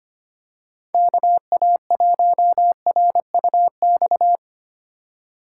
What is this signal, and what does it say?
Morse code, heard as a keyed tone, about 25 wpm, 700 Hz. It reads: KA1RUX
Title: USA Call Signs - 6 Characters - 4x Character Spacing - 25wpm